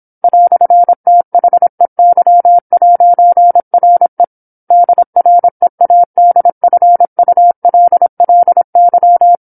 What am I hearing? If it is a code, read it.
Morse code, heard as a keyed tone, about 27 wpm, 700 Hz. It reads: "THEY'RE DREADFULLY
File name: 08-Chapter-08 26 WPM